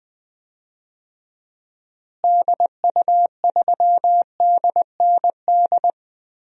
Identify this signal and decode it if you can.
Morse code, heard as a keyed tone, about 20 wpm, 700 Hz. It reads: DU3DND
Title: DXCC-100 Call Signs - 6 Characters - Repeated 2x - 20wpm